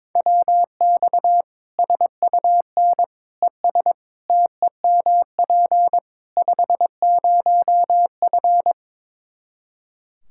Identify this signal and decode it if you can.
Morse code, heard as a keyed tone, about 22 wpm, 700 Hz. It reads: WX SUN ES TEMP 50F
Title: Practice QSO - Exchange 2 - Info - 22wpm